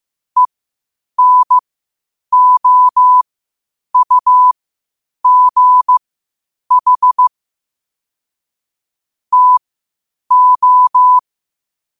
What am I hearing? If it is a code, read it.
Morse code, heard as a keyed tone, about 15 wpm, spaced as for 8 wpm, 1000 Hz. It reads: ENOUGH TO
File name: qotdfiles / 8wpm